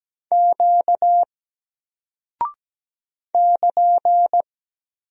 Morse code, heard as a keyed tone, about 17 wpm, 700 Hz. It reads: Q <KN>
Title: Single Character (Letter:Number:Punctuation:Prosign) 17wpm